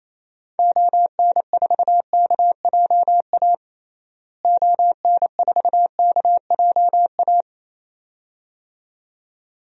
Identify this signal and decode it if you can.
Morse code, heard as a keyed tone, about 28 wpm, 700 Hz. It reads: ON4KJA ON4KJA
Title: DXCC-100 Call Signs - 6 Characters - Repeated 2x - 28wpm